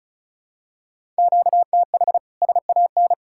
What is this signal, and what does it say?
Morse code, heard as a keyed tone, about 34 wpm, 700 Hz. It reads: QTH SAN
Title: K1USN SST QSO Practice 35wpm